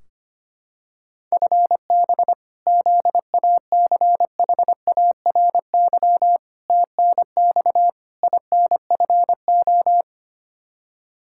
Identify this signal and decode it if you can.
Morse code, heard as a keyed tone, about 25 wpm, 700 Hz. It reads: FB ZACHARY TNX INFO
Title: Practice QSO - Exchange 3 - More Info - 25wpm